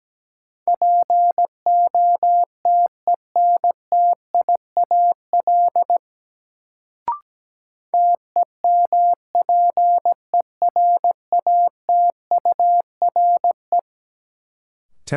Morse code, heard as a keyed tone, about 17 wpm, 700 Hz. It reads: POTENTIAL TEMPERATURE
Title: Sentences from Top 1000 Words - Review Part 4 - New Word Drill - 17wpm